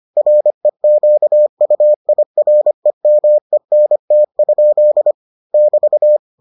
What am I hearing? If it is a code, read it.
Morse code, heard as a keyed tone, about 25 wpm, 600 Hz. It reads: REQUIREMENT? <BT>